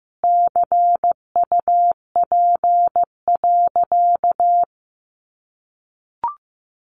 Morse code, heard as a keyed tone, about 15 wpm, 700 Hz. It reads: CUP.
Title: Additional Easy Phrases 15wpm